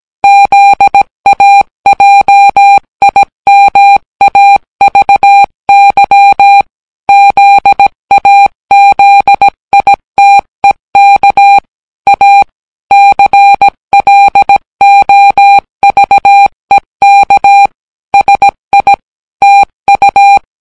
Morse code, heard as a keyed tone, about 17 wpm, 800 Hz. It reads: ZAJIMAVY ZAZITEK A CLOVEK SI TU